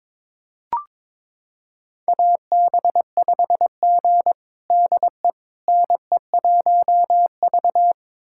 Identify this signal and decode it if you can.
Morse code, heard as a keyed tone, about 22 wpm, 700 Hz. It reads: AB5G DE NE1V
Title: USA Callsign DE Callsign - 4 Characters - 22wpm